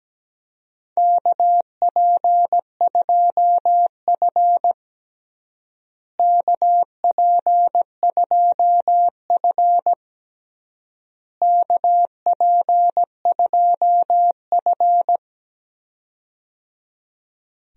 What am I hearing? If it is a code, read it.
Morse code, heard as a keyed tone, about 17 wpm, 700 Hz. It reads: KP2F KP2F KP2F
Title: DXCC-100 Call Signs - 4 Characters - Repeated 3x - 17wpm